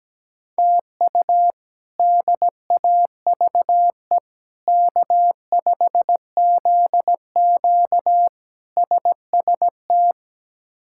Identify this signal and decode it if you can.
Morse code, heard as a keyed tone, about 17 wpm, 700 Hz. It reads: TU DAVE K5ZQ SST
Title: K1USN SST QSO Practice - 17wpm